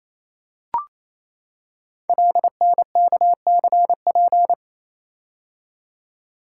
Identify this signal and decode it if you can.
Morse code, heard as a keyed tone, about 27 wpm, 700 Hz. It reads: LNKCP
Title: Code Groups - 5 Letter:Numbers - 28wpm